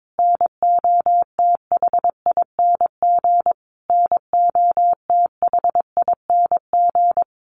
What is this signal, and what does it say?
Morse code, heard as a keyed tone, about 22 wpm, 700 Hz. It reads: NOTHING NOTHING
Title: Top 500 Words Repeated 22wpm